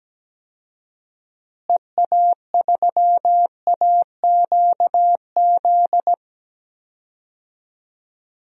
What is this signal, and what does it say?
Morse code, heard as a keyed tone, about 17 wpm, 700 Hz. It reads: EA3AQZ
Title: DXCC-100 Call Signs - 6 Characters - Repeated 2x - 17wpm